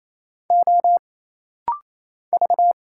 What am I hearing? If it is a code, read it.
Morse code, heard as a keyed tone, about 27 wpm, 700 Hz. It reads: O V
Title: Single Letters - Rapid-Fire 28wpm